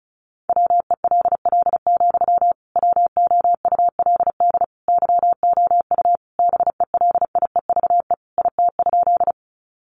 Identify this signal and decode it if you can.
Morse code, heard as a keyed tone, about 34 wpm, 700 Hz. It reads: WELL, WOULD YOU BELIEVE IT?
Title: Sentences from Top 1000 Words 35wpm